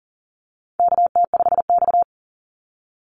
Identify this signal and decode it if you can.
Morse code, heard as a keyed tone, about 38 wpm, 700 Hz. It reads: KT5X
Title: NA SOTA Activator Callsigns 40wpm